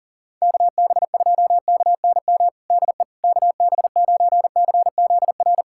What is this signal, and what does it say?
Morse code, heard as a keyed tone, about 40 wpm, 700 Hz. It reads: KB2KNM DE KB9CZR
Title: USA Callsign DE Callsign - 6 Characters - 40wpm